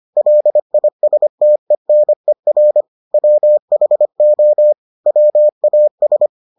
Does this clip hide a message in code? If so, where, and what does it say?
Yes: Morse code from 0.2 to 6.3 s, heard as a keyed tone, about 25 wpm, 600 Hz. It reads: LISTENER WHO WAS